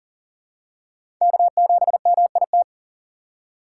Morse code, heard as a keyed tone, about 40 wpm, 700 Hz. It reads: K7MIT
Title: USA Call Signs - 5 Characters - 2x Character Spacing - 40wpm